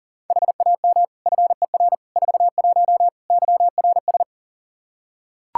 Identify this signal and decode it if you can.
Morse code, heard as a keyed tone, about 40 wpm, 700 Hz. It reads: HAM FER 41 YRS
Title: Practice QSO - Exchange 3 - More Info - 40wpm